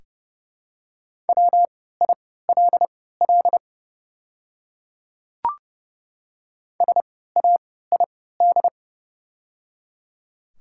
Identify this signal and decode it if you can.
Morse code, heard as a keyed tone, about 29 wpm, spaced as for 17 wpm, 690 Hz. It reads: WILL SAID
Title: 3x Character Spacing - Top 200 Words - 30wpm